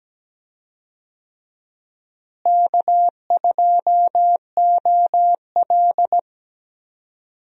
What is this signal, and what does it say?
Morse code, heard as a keyed tone, about 17 wpm, 700 Hz. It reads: K2OL
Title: USA Call Signs - 4 Characters - Repeated 3x - 17wpm